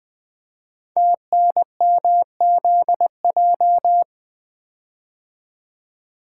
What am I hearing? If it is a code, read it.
Morse code, heard as a keyed tone, about 20 wpm, 700 Hz. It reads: TNMZJ
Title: Code Groups - 5 Letters - 20wpm